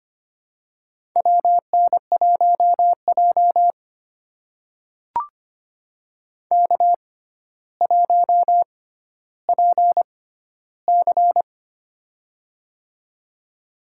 Morse code, heard as a keyed tone, about 25 wpm, spaced as for 13 wpm, 700 Hz. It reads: WN1J K1PC
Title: USA Call Signs - 6x Character Spacing - 25wpm